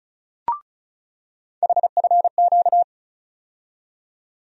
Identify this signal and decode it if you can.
Morse code, heard as a keyed tone, about 36 wpm, 700 Hz. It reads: HFQ
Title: USA Call Sign Postfixes - 3 Letters 35wpm